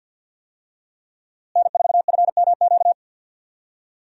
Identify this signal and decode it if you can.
Morse code, heard as a keyed tone, about 52 wpm, 700 Hz. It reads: N4FMQ
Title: USA Call Signs - 5 Characters - Repeated 3x - 50wpm